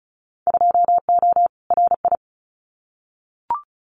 Morse code, heard as a keyed tone, about 34 wpm, 700 Hz. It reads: 2O RI
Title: Winter Field Day Exchanges 35wpm